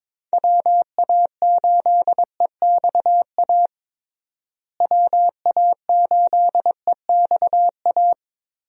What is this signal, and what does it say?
Morse code, heard as a keyed tone, about 22 wpm, 700 Hz. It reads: WA8EXA WA8EXA
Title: USA Call Signs - 6 Characters - Repeated 2x - 22wpm